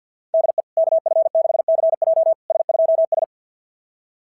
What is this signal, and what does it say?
Morse code, heard as a keyed tone, about 46 wpm, 660 Hz. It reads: DE KF6CJ S2S
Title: SOTA QSO Practice 50wpm